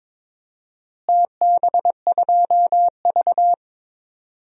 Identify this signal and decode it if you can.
Morse code, heard as a keyed tone, about 21 wpm, 700 Hz. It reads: TB2V
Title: DXCC-100 Call Signs - 4 Characters - Repeated 3x - 22wpm